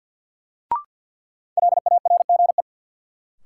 Morse code, heard as a keyed tone, about 47 wpm, 700 Hz. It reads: LARGE
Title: Sets of 4 Words - New Word Drill - 50wpm